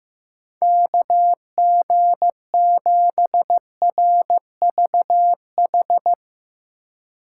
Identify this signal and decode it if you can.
Morse code, heard as a keyed tone, about 15 wpm, 700 Hz. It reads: KG7RVH
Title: USA Call Signs - 6 Characters - 2x Character Spacing - 15wpm